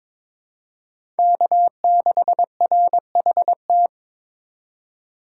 Morse code, heard as a keyed tone, about 22 wpm, 700 Hz. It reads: K6RHT
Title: USA Call Signs - 2x Character Spacing - 22wpm